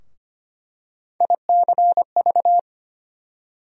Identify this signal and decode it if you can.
Morse code, heard as a keyed tone, about 25 wpm, 700 Hz. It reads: ICV
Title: USA Call Sign Postfixes - 3 Letters 25wpm